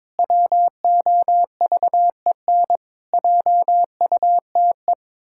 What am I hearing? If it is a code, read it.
Morse code, heard as a keyed tone, about 22 wpm, 700 Hz. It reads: WOVEN JUTE
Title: Pangram Challenge v2 - 22wpm